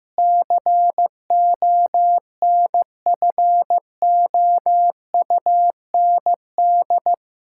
Morse code, heard as a keyed tone, about 15 wpm, 700 Hz. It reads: CONFOUND
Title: Pangram Challenge v2 - 15wpm